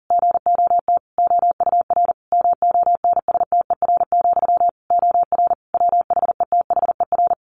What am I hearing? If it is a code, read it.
Morse code, heard as a keyed tone, about 40 wpm, 700 Hz. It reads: GOT OUR MONSTER, OR WHETHER
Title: A Sound Of Thunder 40wpm